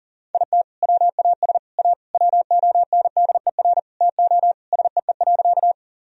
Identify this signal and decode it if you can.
Morse code, heard as a keyed tone, about 40 wpm, 700 Hz. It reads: IT WAS A WONDER TO SEE.
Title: Sentences from Top 1000 Words - Review Part 3 - 40wpm